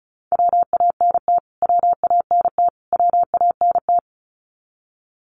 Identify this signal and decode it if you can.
Morse code, heard as a keyed tone, about 34 wpm, 700 Hz. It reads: WANT WANT WANT
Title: Top 500 Words Repeated 35wpm